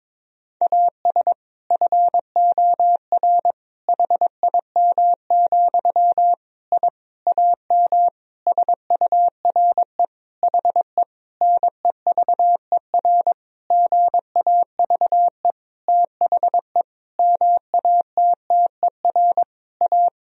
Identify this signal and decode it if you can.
Morse code, heard as a keyed tone, about 22 wpm, 700 Hz. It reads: AS FOR HIM, I AM SURE HE NEVER GAVE THE MATTER A
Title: Sentences from Top 1000 Words - Review Part 2 - 22wpm